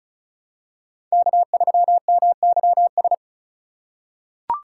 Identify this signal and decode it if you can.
Morse code, heard as a keyed tone, about 36 wpm, 700 Hz. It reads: K3MYS
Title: USA Call Signs - 5 Characters - 3x Character Spacing - 35wpm